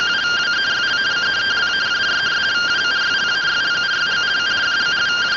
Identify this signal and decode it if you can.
Two-tone teleprinter signal, about 170 Hz shift.